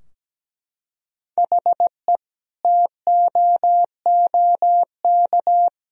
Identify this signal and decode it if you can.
Morse code, heard as a keyed tone, about 17 wpm, 700 Hz. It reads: HE TOOK
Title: Sentences from Top 1000 Words - Review Part 2 - 17wpm